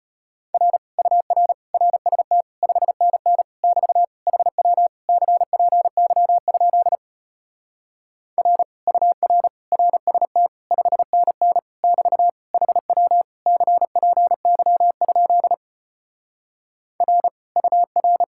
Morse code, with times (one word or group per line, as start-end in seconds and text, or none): 0.5-0.8 R
1.0-1.5 UR
1.7-2.4 RST
2.6-3.4 5NN
3.6-4.0 <BT>
4.3-4.9 HW
5.1-7.0 CPY?
8.4-8.6 R
8.9-9.5 UR
9.7-10.5 RST
10.7-11.6 5NN
11.8-12.3 <BT>
12.5-13.2 HW
13.5-15.5 CPY?
17.0-17.3 R
17.6-18.2 UR